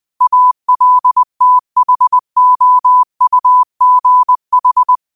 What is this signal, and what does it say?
Morse code, heard as a keyed tone, about 20 wpm, 1000 Hz. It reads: ALTHOUGH
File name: qotdfiles / 20wpm